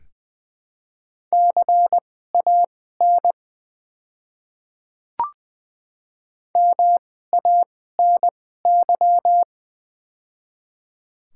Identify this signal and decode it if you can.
Morse code, heard as a keyed tone, about 20 wpm, spaced as for 15 wpm, 700 Hz. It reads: CAN MANY
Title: 2x Character Spacing - Top 500 Words - 20wpm